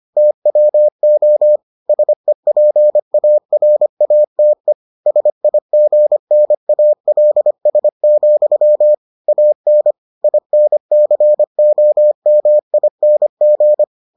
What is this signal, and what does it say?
Morse code, heard as a keyed tone, about 25 wpm, 600 Hz. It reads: TWO SEPARATE SIGNALS, AN INCOMING